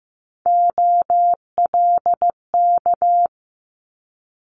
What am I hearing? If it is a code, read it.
Morse code, heard as a keyed tone, about 15 wpm, 700 Hz. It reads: OLK
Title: Multi-letter Phonemes Quick Fire 15wpm